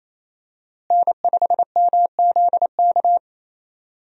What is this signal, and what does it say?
Morse code, heard as a keyed tone, about 28 wpm, 700 Hz. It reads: N5MZK
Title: USA Call Signs - 5 Characters - Repeated 2x - 28wpm